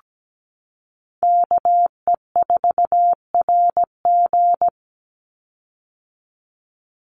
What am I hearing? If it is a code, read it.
Morse code, heard as a keyed tone, about 17 wpm, 700 Hz. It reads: KE4RG
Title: Call Signs - 17wpm